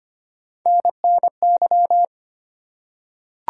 Morse code, heard as a keyed tone, about 25 wpm, 700 Hz. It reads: NNY
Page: morse-code-ninja-practice-25wpm